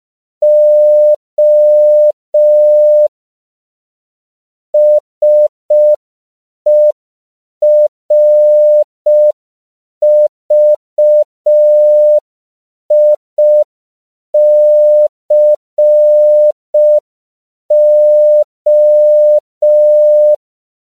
Morse code, heard as a keyed tone, about 5 wpm, 600 Hz. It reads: O SERVICO
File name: CW / 5PPM